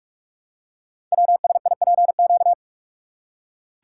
Morse code, heard as a keyed tone, about 44 wpm, 700 Hz. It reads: WSIPQ
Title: Code Groups - 5 Letters - 45wpm